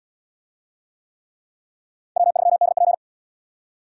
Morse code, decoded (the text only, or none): W3DP